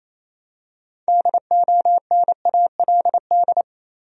Morse code, heard as a keyed tone, about 28 wpm, 700 Hz. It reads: DONALD
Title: USA Common Names - 28wpm